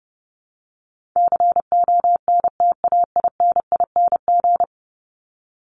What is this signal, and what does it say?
Morse code, heard as a keyed tone, about 30 wpm, 700 Hz. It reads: CONTAINING
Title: ING Suffix - Top 100 Words with Base Words 30wpm